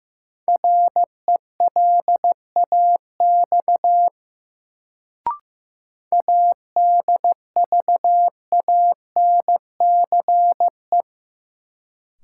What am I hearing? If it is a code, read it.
Morse code, heard as a keyed tone, about 15 wpm, 700 Hz. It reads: RELAX ADVANCE
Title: Sentences from Top 1400 Words - New Word Drill - 15wpm